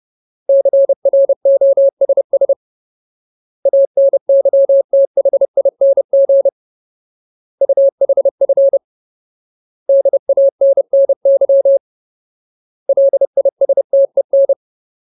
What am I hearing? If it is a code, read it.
Morse code, heard as a keyed tone, about 30 wpm, 550 Hz. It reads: CROSS ANYTHING UHF DANNY LISTEN
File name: Ultimate Word List / 30WPM Ultimate